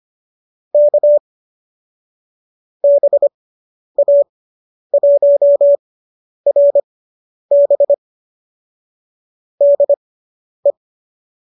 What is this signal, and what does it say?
Morse code, heard as a keyed tone, about 25 wpm, spaced as for 10 wpm, 590 Hz. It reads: K BA1RB DE